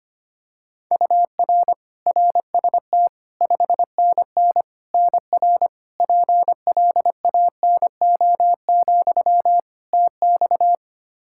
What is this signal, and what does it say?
Morse code, heard as a keyed tone, about 25 wpm, 700 Hz. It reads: UR RST 5NN NR PLANO, TX
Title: Practice QSO - Exchange 1 - Beginning - 25wpm